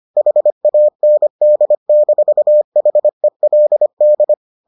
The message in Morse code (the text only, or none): HAND-HELD